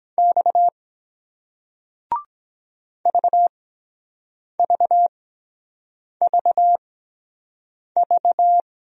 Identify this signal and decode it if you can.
Morse code, heard as a keyed tone, about 23 wpm, spaced as for 7 wpm, 700 Hz. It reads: X VVVV